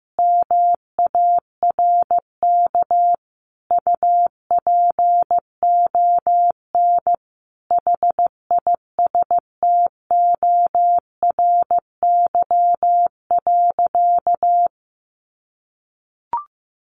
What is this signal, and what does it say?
Morse code, heard as a keyed tone, about 15 wpm, 700 Hz. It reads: MARK UPON HISTORY.
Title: Sentences from Top 1100 Words 15wpm